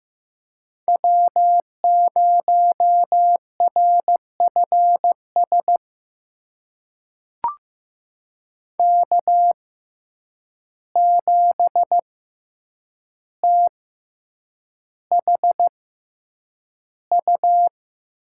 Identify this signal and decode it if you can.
Morse code, heard as a keyed tone, about 15 wpm, spaced as for 8 wpm, 700 Hz. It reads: W0RFS K7THU
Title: USA Call Signs - 5 Characters - 6x Character Spacing - 15wpm